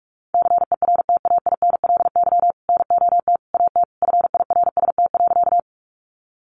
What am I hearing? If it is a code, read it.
Morse code, heard as a keyed tone, about 44 wpm, 700 Hz. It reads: CERTAINLY NOT AT FIRST.
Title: Sentences from Top 900 Words 45wpm